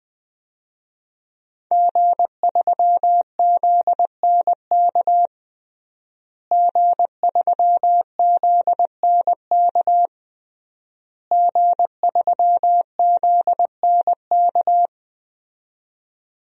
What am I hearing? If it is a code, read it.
Morse code, heard as a keyed tone, about 21 wpm, 700 Hz. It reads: G3ZNK G3ZNK G3ZNK
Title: DXCC-100 Call Signs - 5 Characters - Repeated 3x - 20wpm